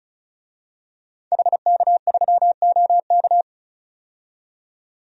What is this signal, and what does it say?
Morse code, heard as a keyed tone, about 34 wpm, 700 Hz. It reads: HK3OK
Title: DXCC-100 Call Signs - 5 Characters - 35wpm